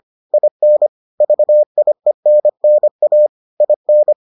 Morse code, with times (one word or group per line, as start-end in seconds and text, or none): 0.3-0.9 IN
1.2-3.3 VIENNA
3.6-4.1 IN